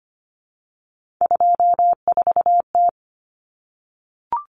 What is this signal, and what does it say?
Morse code, heard as a keyed tone, about 25 wpm, 700 Hz. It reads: 24T
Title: Common Cut Numbers - 3-Digit 25wpm